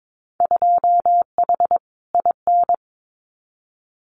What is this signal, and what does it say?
Morse code, heard as a keyed tone, about 22 wpm, 700 Hz. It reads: 2H IN
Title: Winter Field Day Exchanges Encore 22wpm